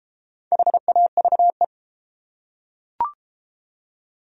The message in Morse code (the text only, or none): HAVE